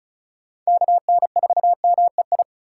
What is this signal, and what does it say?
Morse code, heard as a keyed tone, about 36 wpm, 700 Hz. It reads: KN4MEI
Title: USA Call Signs - 2x Character Spacing - 35wpm